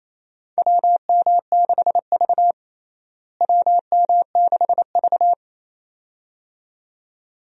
Morse code, heard as a keyed tone, about 28 wpm, 700 Hz. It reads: WM6V WM6V
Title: USA Call Signs - 4 Characters - Repeated 2x - 28wpm